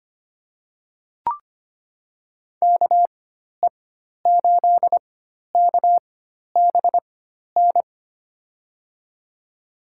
Morse code, heard as a keyed tone, about 25 wpm, spaced as for 12 wpm, 700 Hz. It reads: KE8KBN
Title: USA Call Signs - 6 Characters - 4x Character Spacing - 25wpm